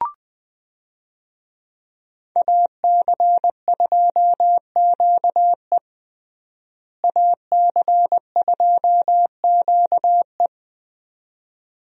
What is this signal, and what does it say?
Morse code, heard as a keyed tone, about 20 wpm, 700 Hz. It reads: AC2QE AC2QE
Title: USA Call Signs - 5 Characters - Repeated 2x - 20wpm